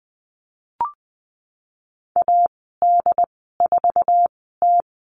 Morse code, heard as a keyed tone, about 20 wpm, spaced as for 15 wpm, 700 Hz. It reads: AD4T
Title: 088 Lesson 15g - Number 5 - Call Signs 20wpm